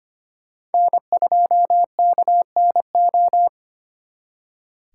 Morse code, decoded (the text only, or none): N2KNO